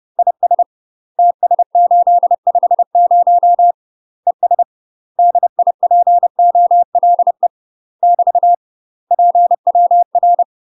Morse code, heard as a keyed tone, about 30 wpm, 700 Hz. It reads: IS TS850 ES DIPOLE <BT> PWR
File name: UK MASTER QSOs TRAINING / 30WPM / UK MASTER QSOs 0042